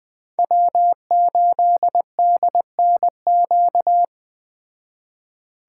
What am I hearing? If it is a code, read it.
Morse code, heard as a keyed tone, about 20 wpm, 700 Hz. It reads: W8DNQ